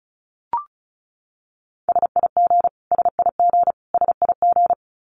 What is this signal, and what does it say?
Morse code, heard as a keyed tone, about 34 wpm, 700 Hz. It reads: SIG SIG SIG
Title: Abbreviations Repeated 3x - 35wpm